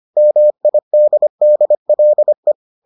Morse code, heard as a keyed tone, about 25 wpm, 600 Hz. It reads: MIDDLE